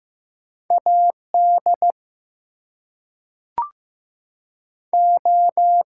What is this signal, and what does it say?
Morse code, heard as a keyed tone, about 15 wpm, 700 Hz. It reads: AD O
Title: Two Letter Words - 15wpm